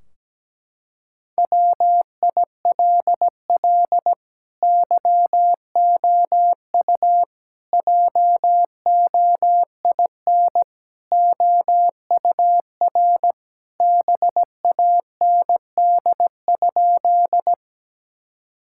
Morse code, heard as a keyed tone, about 17 wpm, 700 Hz. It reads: WILL YOU JOIN OUR BAND?
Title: Sentences from Top 1600 Words 17wpm